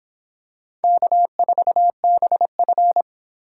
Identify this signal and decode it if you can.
Morse code, heard as a keyed tone, about 24 wpm, 700 Hz. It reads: K4BF